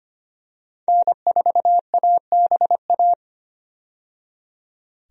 Morse code, heard as a keyed tone, about 25 wpm, 690 Hz. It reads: N4ABA
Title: USA Call Signs - 5 Characters - 25wpm